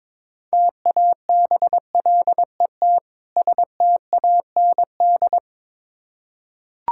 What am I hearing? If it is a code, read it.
Morse code, heard as a keyed tone, about 22 wpm, 690 Hz. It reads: TABLET STAND
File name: Difficult Words with Hints - 22wpm